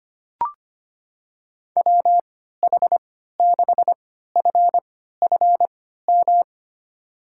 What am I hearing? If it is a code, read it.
Morse code, heard as a keyed tone, about 25 wpm, spaced as for 14 wpm, 700 Hz. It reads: WH6FFM
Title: USA Call Signs - 6 Characters - 3x Character Spacing - 25wpm